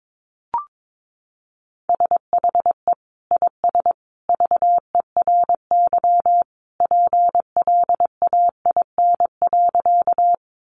Morse code, heard as a keyed tone, about 22 wpm, 700 Hz. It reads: SHE IS VERY PLAIN.
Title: Sentences from Top 400 Words 22wpm